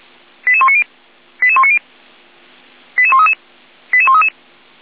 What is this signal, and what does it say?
Five-tone selective call (ZVEI), heard as quick runs of five tones, tones of about 70 ms. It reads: 88190 88190 88133 88133
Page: SDR/ZVEI